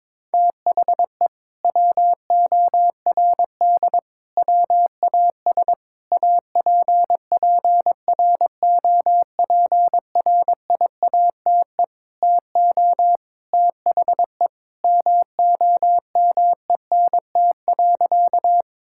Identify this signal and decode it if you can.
Morse code, heard as a keyed tone, about 21 wpm, 700 Hz. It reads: THE WORD WAS APPROPRIATE TO THE MOMENT.
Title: Sentences from Top 1000 Words - Review Part 5 - 22wpm